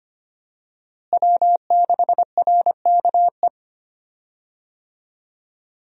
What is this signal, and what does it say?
Morse code, heard as a keyed tone, about 25 wpm, 700 Hz. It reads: W6RKE
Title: USA Call Signs - 5 Characters - Repeated 3x - 25wpm